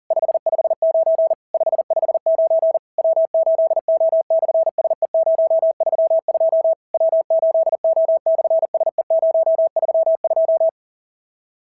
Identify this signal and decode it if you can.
Morse code, heard as a keyed tone, about 40 wpm, 660 Hz. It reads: 559 559 W8O/SE032 W8O/SE032
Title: SOTA QSO Practice 40wpm